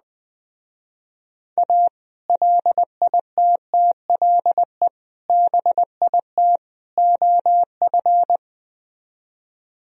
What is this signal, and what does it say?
Morse code, heard as a keyed tone, about 20 wpm, 700 Hz. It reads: A LITTLE BIT OF
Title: Speed-Racing - Sets of 4 Words 17wpm